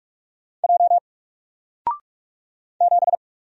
Morse code, heard as a keyed tone, about 44 wpm, 700 Hz. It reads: J 7
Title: Single Letter-Number Alt 45wpm